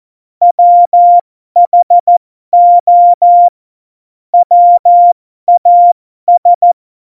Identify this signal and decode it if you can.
Morse code, heard as a keyed tone, about 14 wpm, spaced as for 12 wpm, 700 Hz. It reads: WHO WAS
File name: Morsekurs / Witze